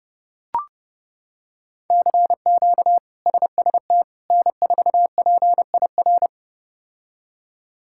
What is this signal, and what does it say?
Morse code, heard as a keyed tone, about 31 wpm, 700 Hz. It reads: CQ SST N4PIR
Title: K1USN SST QSO Practice - 30wpm